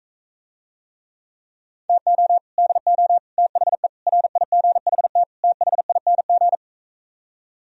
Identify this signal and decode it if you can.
Morse code, heard as a keyed tone, about 45 wpm, 700 Hz. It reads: TO DO THE RIGHT THING